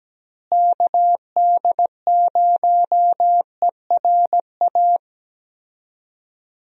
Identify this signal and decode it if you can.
Morse code, heard as a keyed tone, about 17 wpm, 700 Hz. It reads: KD0ERA
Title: USA Call Signs - 6 Characters - Repeated 2x - 17wpm